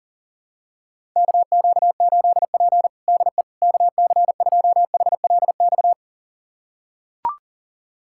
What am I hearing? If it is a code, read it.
Morse code, heard as a keyed tone, about 40 wpm, 700 Hz. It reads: KQ8P DE KC2HLX
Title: USA Callsign DE Callsign 40wpm